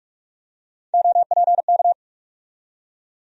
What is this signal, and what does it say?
Morse code, heard as a keyed tone, about 42 wpm, 700 Hz. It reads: OPK
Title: USA Call Sign Postfixes - 3 Letters 45wpm